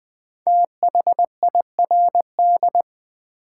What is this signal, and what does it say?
Morse code, heard as a keyed tone, about 20 wpm, 690 Hz. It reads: THIRD
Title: 5 Letter Words - First 3 Letters Given - 20wpm